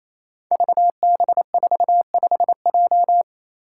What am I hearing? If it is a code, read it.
Morse code, heard as a keyed tone, about 28 wpm, 700 Hz. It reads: VB45J
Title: Code Groups - 5 Letter:Numbers - 28wpm